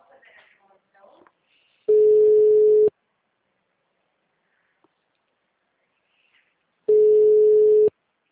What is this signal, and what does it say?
Ringback tone (Europe, 425 Hz), with speech.
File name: out- -2009- - - .48927